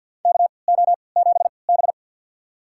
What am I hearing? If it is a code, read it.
Morse code, heard as a keyed tone, about 50 wpm, spaced as for 28 wpm, 700 Hz. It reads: KO7B
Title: USA Call Signs - 3x Character Spacing - 50wpm